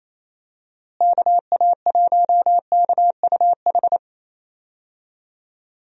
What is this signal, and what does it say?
Morse code, heard as a keyed tone, about 28 wpm, 700 Hz. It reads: KA1KUH
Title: USA Call Signs - 6 Characters - Repeated 3x - 28wpm